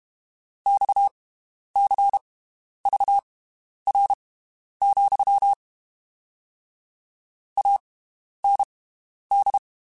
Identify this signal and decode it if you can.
Morse code, heard as a keyed tone, about 32 wpm, spaced as for 11 wpm, 800 Hz. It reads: XCVR, AND
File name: (spaced) / QSOs Spaced-mp3s / WD0FBT 1 @30 sp